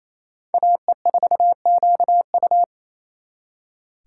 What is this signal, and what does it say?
Morse code, heard as a keyed tone, about 34 wpm, 700 Hz. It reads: AE4QU
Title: Speed Racing - USA Call Signs 28wpm